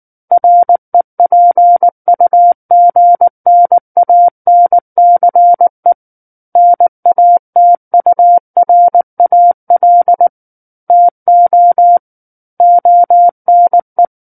Morse code, heard as a keyed tone, about 19 wpm, 700 Hz. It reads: REPUGNANCE NATURAL TO ONE
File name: Book One / 10-Chapter 10 - 19 WPM